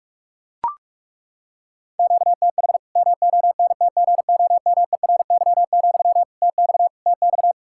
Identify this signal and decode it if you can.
Morse code, heard as a keyed tone, about 41 wpm, 700 Hz. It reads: QTH MONTGOMERY, TX TX
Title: Practice QSO - Exchange 1 - Beginning - 45wpm